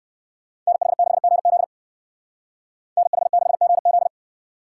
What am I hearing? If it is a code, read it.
Morse code, heard as a keyed tone, about 58 wpm, 700 Hz. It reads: NH6GZ NH6GZ